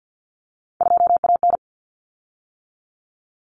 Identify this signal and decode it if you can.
Morse code, heard as a keyed tone, about 50 wpm, 700 Hz. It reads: 2AN
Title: Common Cut Numbers - 3-Digit 50wpm